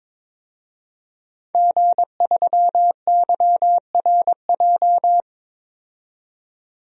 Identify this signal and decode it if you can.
Morse code, heard as a keyed tone, about 22 wpm, 700 Hz. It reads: G3YRJ
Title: DXCC-100 Call Signs - 5 Characters - 22wpm